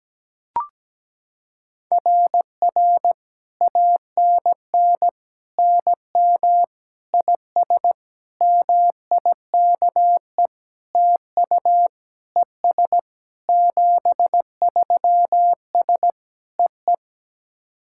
Morse code, heard as a keyed tone, about 17 wpm, 700 Hz. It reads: RR ANN NM IS MIKE TU ES 73S EE
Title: SOTA QSO Practice 17wpm